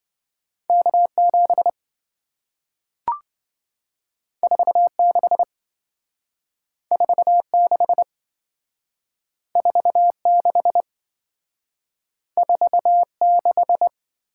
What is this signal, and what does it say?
Morse code, heard as a keyed tone, about 27 wpm, 700 Hz. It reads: K7 46 46 46 46